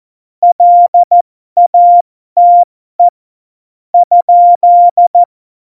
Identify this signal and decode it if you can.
Morse code, heard as a keyed tone, about 14 wpm, spaced as for 12 wpm, 700 Hz. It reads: LATE ?
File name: Morsekurs / Witze